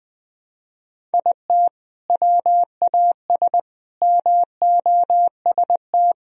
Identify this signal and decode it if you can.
Morse code, heard as a keyed tone, about 21 wpm, 700 Hz. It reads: IT WAS MOST
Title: Sentences from Top 1000 Words - Review Part 1 - 20wpm